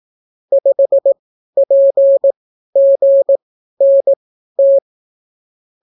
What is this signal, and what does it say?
Morse code, heard as a keyed tone, about 18 wpm, spaced as for 12 wpm, 550 Hz. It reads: 5PGNT